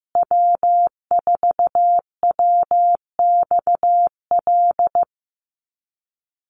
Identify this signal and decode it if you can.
Morse code, heard as a keyed tone, about 15 wpm, 700 Hz. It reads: W4WXL
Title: USA Call Signs Repeated 2x 15wpm